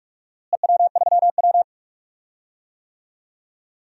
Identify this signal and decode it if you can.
Morse code, heard as a keyed tone, about 43 wpm, 700 Hz. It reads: EW3W
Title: DXCC-100 Call Signs - 4 Characters - Repeated 2x - 45wpm